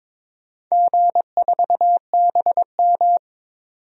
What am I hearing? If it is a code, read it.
Morse code, heard as a keyed tone, about 22 wpm, 700 Hz. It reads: G4BM
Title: DXCC-100 Call Signs - 4 Characters - 22wpm